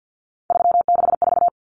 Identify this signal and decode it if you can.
Morse code, heard as a keyed tone, about 50 wpm, 700 Hz. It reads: 364
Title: Common Cut Numbers - 3-Digit 50wpm